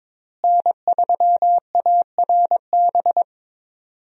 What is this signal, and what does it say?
Morse code, heard as a keyed tone, about 22 wpm, 700 Hz. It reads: N3ARB
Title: USA Call Signs - 5 Characters - Repeated 2x - 22wpm